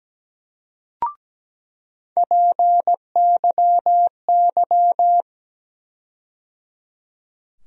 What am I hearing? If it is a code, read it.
Morse code, heard as a keyed tone, about 17 wpm, 700 Hz. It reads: PYY
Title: USA Call Sign Postfixes - 3 Letters 17wpm